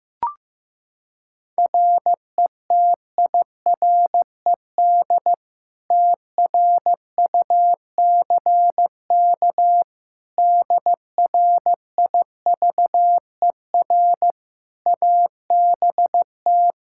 Morse code, heard as a keyed tone, about 15 wpm, 700 Hz. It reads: RETIRED TRUCK DRIVER ABT
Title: Practice QSO - Exchange 3 - More Info - 15wpm